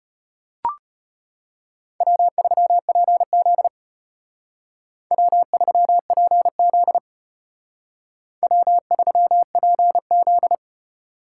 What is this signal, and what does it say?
Morse code, heard as a keyed tone, about 30 wpm, 700 Hz. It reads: W3PZ W3PZ W3PZ
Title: Speed Racing - USA 1x2 Call Signs 25wpm